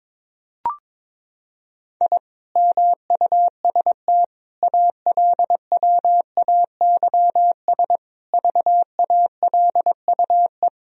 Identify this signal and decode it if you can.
Morse code, heard as a keyed tone, about 21 wpm, 700 Hz. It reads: I MUST ALWAYS VALUE
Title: Sentences from Top 1000 Words - Review Part 3 - 22wpm